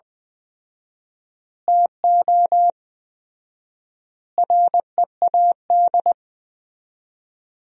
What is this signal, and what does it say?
Morse code, heard as a keyed tone, about 21 wpm, 700 Hz. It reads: TO READ
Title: 4x Word-Spacing - Sentences from Top 500 Words - 20wpm